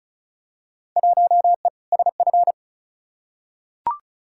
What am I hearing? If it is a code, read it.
Morse code, heard as a keyed tone, about 36 wpm, 700 Hz. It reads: 1E SF
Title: Field Day Exchanges - 35wpm